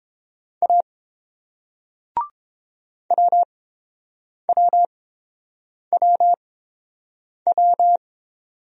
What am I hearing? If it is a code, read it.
Morse code, heard as a keyed tone, about 30 wpm, spaced as for 7 wpm, 700 Hz. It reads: A WWWW